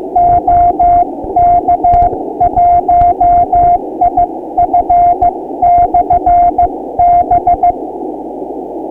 Morse code, heard as a keyed tone, about 15 wpm, 700 Hz, with noise. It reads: OK1IF/B
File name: ARCHIV / Beacon 1999.65 / 2015 03 13small